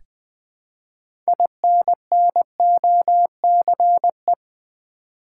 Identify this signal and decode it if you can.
Morse code, heard as a keyed tone, about 20 wpm, 700 Hz. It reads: INNOCE
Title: Build A Word - 3 Letters At A Time - 8-Letter Words - 20wpm